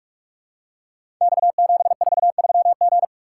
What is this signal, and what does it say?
Morse code, heard as a keyed tone, about 44 wpm, 690 Hz. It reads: X743G
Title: Code Groups - 5 Letter:Numbers - 45wpm